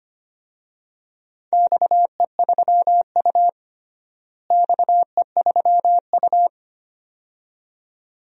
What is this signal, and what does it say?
Morse code, heard as a keyed tone, about 25 wpm, 700 Hz. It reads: XE3U XE3U
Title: DXCC-100 Call Signs - 4 Characters - Repeated 2x - 25wpm